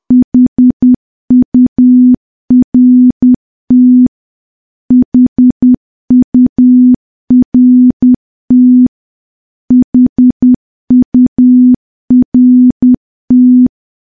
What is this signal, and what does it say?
Morse code, heard as a keyed tone, about 10 wpm, 250 Hz. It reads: HURT HURT HURT
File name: morse / morse-training-10WPM